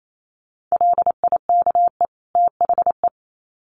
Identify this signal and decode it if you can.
Morse code, heard as a keyed tone, about 28 wpm, 700 Hz. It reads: LIKE THE
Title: Sets of 2 Words 28wpm